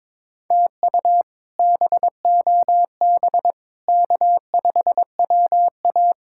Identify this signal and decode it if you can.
Morse code, heard as a keyed tone, about 22 wpm, 700 Hz. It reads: TU BOB K5WA
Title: K1USN SST QSO Practice - 22wpm